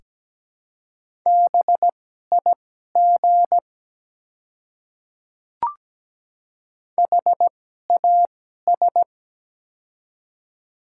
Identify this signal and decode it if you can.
Morse code, heard as a keyed tone, about 17 wpm, spaced as for 12 wpm, 700 Hz. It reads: BIG HAS
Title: 2x Character Spacing - Top 300 Words - 17wpm